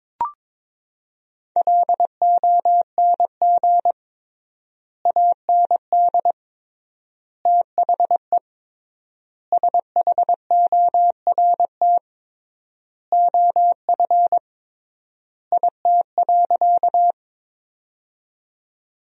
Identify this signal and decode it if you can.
Morse code, heard as a keyed tone, about 21 wpm, 700 Hz. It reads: LONG AND THE SHORT OF IT.
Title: 3x Word-Spacing - Sentences from Top 400 Words - 22wpm